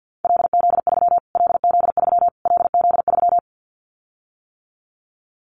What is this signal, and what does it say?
Morse code, heard as a keyed tone, about 50 wpm, 700 Hz. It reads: LZ3 LZ3 LZ3
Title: Callsign Prefixes for Top 25 DXCC - Repeated 3x - 50wpm